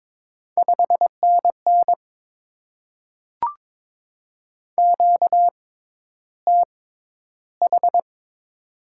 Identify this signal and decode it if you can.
Morse code, heard as a keyed tone, about 22 wpm, spaced as for 11 wpm, 690 Hz. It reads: 5NN QTH
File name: Top 20 QSO Elements - 6x Character Spacing - 22wpm